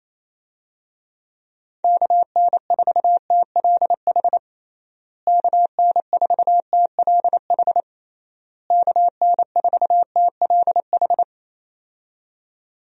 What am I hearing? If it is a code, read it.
Morse code, heard as a keyed tone, about 27 wpm, 700 Hz. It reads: KN4TLH KN4TLH KN4TLH
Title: USA Call Signs - 6 Characters - Repeated 3x - 28wpm